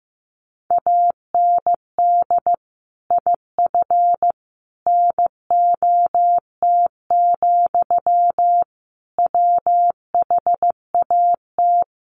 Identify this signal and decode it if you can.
Morse code, heard as a keyed tone, about 15 wpm, 700 Hz. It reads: AND IF NOT, WHAT
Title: Sentences from Top 1400 Words 15wpm